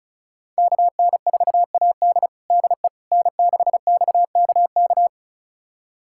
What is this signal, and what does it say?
Morse code, heard as a keyed tone, about 36 wpm, 700 Hz. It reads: KN4AD DE N6XKK
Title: USA Callsign DE Callsign - 5 Characters - 35wpm